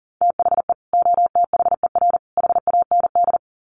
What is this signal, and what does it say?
Morse code, heard as a keyed tone, about 40 wpm, 700 Hz. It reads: THE OTHER HAND
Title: Sets of 3 Words 40wpm